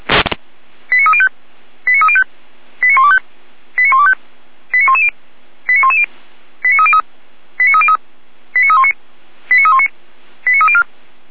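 Five-tone selective call (ZVEI), heard as quick runs of five tones, tones of about 70 ms. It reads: 89305 89305 89125 89125 89110 89110 89303 89303 89319 89319 89304
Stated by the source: ZVEI five-tone selective call 89097 received with an SDR.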